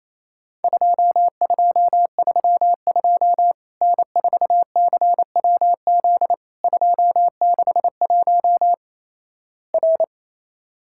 Morse code, heard as a keyed tone, about 28 wpm, 700 Hz. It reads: 2232 N4CWZ 261 R
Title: Worked All Europe DX Contest - QTC Exchange - 28wpm